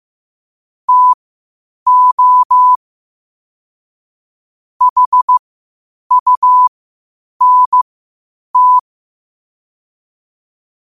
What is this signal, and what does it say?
Morse code, heard as a keyed tone, about 15 wpm, spaced as for 8 wpm, 1000 Hz. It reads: TO HUNT
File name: qotdfiles / 8wpm